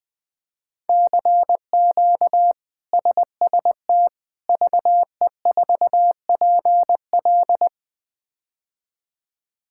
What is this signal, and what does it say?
Morse code, heard as a keyed tone, about 20 wpm, 700 Hz. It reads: CQ SST VE4PL
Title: K1USN SST QSO Practice - 20wpm